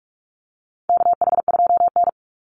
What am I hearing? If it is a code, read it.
Morse code, heard as a keyed tone, about 44 wpm, 700 Hz. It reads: KH2N
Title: Call Signs - 45wpm